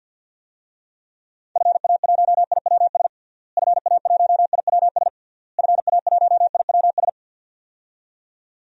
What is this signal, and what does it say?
Morse code, heard as a keyed tone, about 46 wpm, 700 Hz. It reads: FA1IWS FA1IWS FA1IWS
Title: DXCC-100 Call Signs - 6 Characters - Repeated 3x - 50wpm